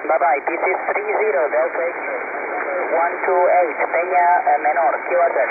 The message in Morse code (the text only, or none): none